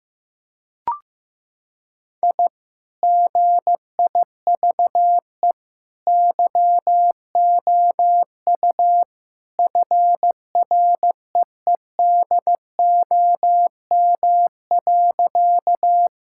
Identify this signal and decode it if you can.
Morse code, heard as a keyed tone, about 15 wpm, 700 Hz. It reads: I GIVE YOU FREEDOM.
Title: Sentences from Top 1000 Words - Review Part 5 - 15wpm